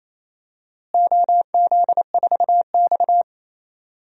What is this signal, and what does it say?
Morse code, heard as a keyed tone, about 27 wpm, 700 Hz. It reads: OZ4X
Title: DXCC-100 Call Signs - 4 Characters - Repeated 3x - 28wpm